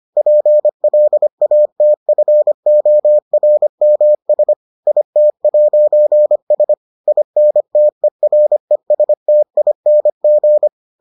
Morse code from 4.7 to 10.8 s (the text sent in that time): IT'S INTERESTING